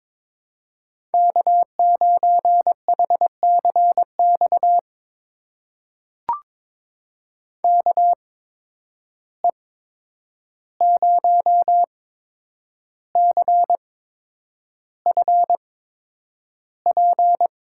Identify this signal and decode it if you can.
Morse code, heard as a keyed tone, about 22 wpm, spaced as for 9 wpm, 700 Hz. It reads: K9HCX KE0CFP
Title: USA Call Signs - 8x Character Spacing - 22wpm